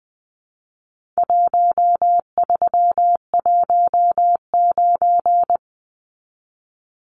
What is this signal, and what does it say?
Morse code, heard as a keyed tone, about 20 wpm, 700 Hz. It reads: 1319
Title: Numbers - Contest Style - 4-Digit 20wpm